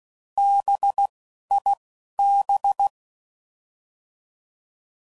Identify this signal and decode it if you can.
Morse code, heard as a keyed tone, about 16 wpm, 800 Hz. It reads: B I B